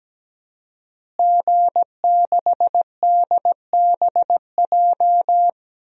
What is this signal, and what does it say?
Morse code, heard as a keyed tone, about 17 wpm, 700 Hz. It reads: G6DBJ
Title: DXCC-100 Call Signs - 5 Characters - Repeated 3x - 17wpm